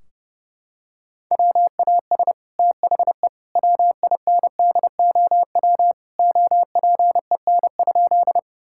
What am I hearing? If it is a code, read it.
Morse code, heard as a keyed tone, about 30 wpm, 700 Hz. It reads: WAS THE WINDOW OPEN?
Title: Sentences from Top 1600 Words 30wpm